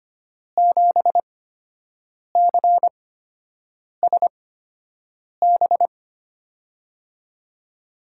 Morse code, heard as a keyed tone, about 25 wpm, spaced as for 7 wpm, 700 Hz. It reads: 7CSB
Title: USA Call Signs - 6 Characters - 8x Character Spacing - 25wpm